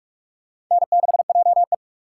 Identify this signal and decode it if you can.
Morse code, heard as a keyed tone, about 47 wpm, 700 Hz. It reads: N6JE
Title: USA Call Signs - 4 Characters - 6x Character Spacing - 45wpm